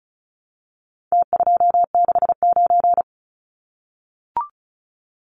Morse code, heard as a keyed tone, about 34 wpm, 700 Hz. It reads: T269
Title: Numbers - Contest Style - 4-Digit 35wpm